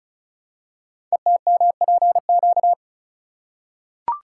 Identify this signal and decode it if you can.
Morse code, heard as a keyed tone, about 36 wpm, 700 Hz. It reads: ETMPQ
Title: Code Groups - 5 Letters - 35wpm